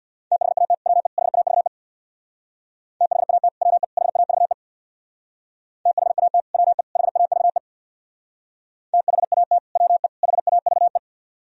Morse code, heard as a keyed tone, about 64 wpm, 700 Hz. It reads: THAT WE HAVE THAT WE HAVE THAT WE HAVE THAT WE HAVE